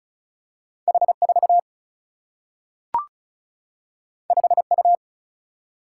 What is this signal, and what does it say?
Morse code, heard as a keyed tone, about 36 wpm, 700 Hz. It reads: H4 5U
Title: Single-Letter Single-Number 35wpm